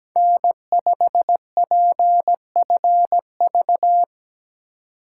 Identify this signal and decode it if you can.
Morse code, heard as a keyed tone, about 17 wpm, 700 Hz. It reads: N5PFV
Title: USA Call Signs - 2x Character Spacing - 17wpm